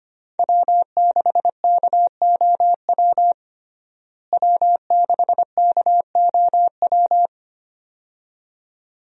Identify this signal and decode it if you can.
Morse code, heard as a keyed tone, about 25 wpm, 700 Hz. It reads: W6KOW W6KOW
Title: USA Call Signs - 5 Characters - Repeated 2x - 25wpm